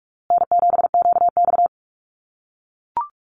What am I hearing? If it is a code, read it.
Morse code, heard as a keyed tone, about 44 wpm, 700 Hz. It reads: N7QX
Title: USA Call Signs Repeated 3x 45wpm